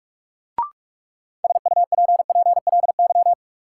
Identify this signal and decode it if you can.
Morse code, heard as a keyed tone, about 46 wpm, 700 Hz. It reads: SUPPLY
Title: Sentences from Top 1000 Words - Review Part 4 - New Word Drill - 45wpm